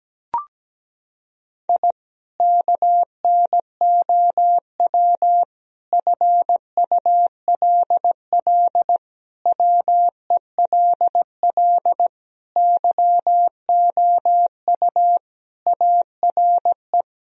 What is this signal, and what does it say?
Morse code, heard as a keyed tone, about 17 wpm, 700 Hz. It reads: I KNOW FULL WELL YOU ARE
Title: Sentences from Top 1000 Words - Review Part 3 - 17wpm